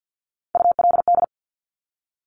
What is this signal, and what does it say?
Morse code, heard as a keyed tone, about 50 wpm, 700 Hz. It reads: ULD
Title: Top Three Letter Combinations in Words 50wpm